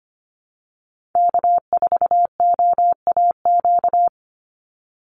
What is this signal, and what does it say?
Morse code, heard as a keyed tone, about 25 wpm, 700 Hz. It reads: K4OAQ
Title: Call Signs Repeated 3x - 25wpm